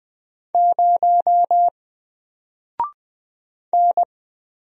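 Morse code, heard as a keyed tone, about 20 wpm, 700 Hz. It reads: 0 N
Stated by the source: Single letter and number Morse drill at 20 wpm, from N4LCD.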